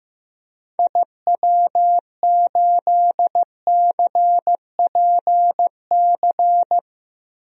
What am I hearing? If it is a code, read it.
Morse code, heard as a keyed tone, about 15 wpm, 700 Hz. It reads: IW8CPC
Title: DXCC-100 Call Signs - 6 Characters - 15wpm